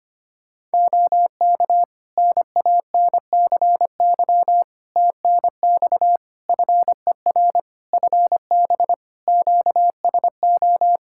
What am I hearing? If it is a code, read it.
Morse code, heard as a keyed tone, about 25 wpm, 700 Hz. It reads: OK NANCY TNX FER FB QSO
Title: Practice QSO - Exchange 4 - Ending - 25wpm